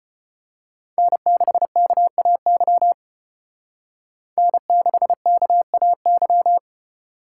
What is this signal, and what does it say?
Morse code, heard as a keyed tone, about 29 wpm, 700 Hz. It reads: N6KAY N6KAY
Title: Speed Racing - USA Call Signs 25wpm